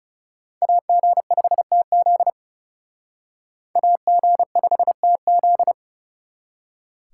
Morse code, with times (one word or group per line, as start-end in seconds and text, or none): 0.6-2.3 AG5TZ
3.8-5.7 AG5TZ